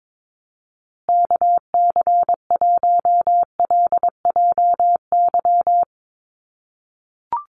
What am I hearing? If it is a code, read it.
Morse code, heard as a keyed tone, about 22 wpm, 700 Hz. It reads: KC1LJY
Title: USA Call Signs Repeated 3x 22wpm